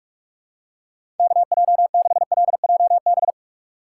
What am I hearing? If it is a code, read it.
Morse code, heard as a keyed tone, about 46 wpm, 700 Hz. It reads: KJ6LJB
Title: USA Call Signs - 6 Characters - 2x Character Spacing - 45wpm